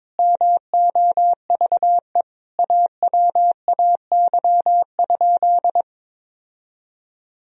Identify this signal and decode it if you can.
Morse code, heard as a keyed tone, about 22 wpm, 700 Hz. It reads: MOVE AWAY?
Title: Sentences from Top 1000 Words - Review Part 1 - 22wpm